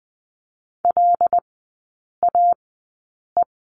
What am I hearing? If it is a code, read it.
Morse code, heard as a keyed tone, about 20 wpm, spaced as for 8 wpm, 700 Hz. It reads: LAE
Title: 074 Lesson 13e - Letter L - Three Characters 20wpm